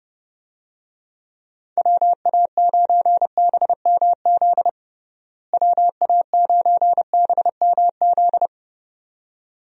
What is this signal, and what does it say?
Morse code, heard as a keyed tone, about 31 wpm, 700 Hz. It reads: WA9BMZ WA9BMZ
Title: USA Call Signs - 6 Characters - Repeated 2x - 30wpm